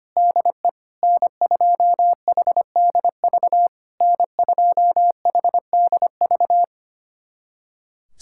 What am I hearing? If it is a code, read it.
Morse code, heard as a keyed tone, about 25 wpm, 700 Hz. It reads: DE N2HDV N2HDV
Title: SOTA QSO Practice 25wpm